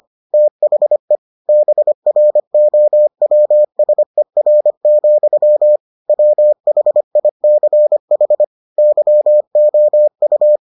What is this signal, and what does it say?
Morse code, heard as a keyed tone, about 25 wpm, 600 Hz. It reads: THE BROWSER, WHICH YOU